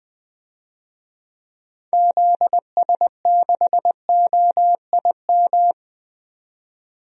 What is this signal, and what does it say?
Morse code, heard as a keyed tone, about 21 wpm, 700 Hz. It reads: ZS6OIM
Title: DXCC-100 Call Signs - 6 Characters - 20wpm